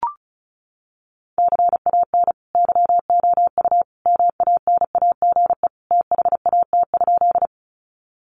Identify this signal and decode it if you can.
Morse code, heard as a keyed tone, about 34 wpm, 700 Hz. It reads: CAN YOU MANAGE THAT?
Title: Sentences from Top 1400 Words 35wpm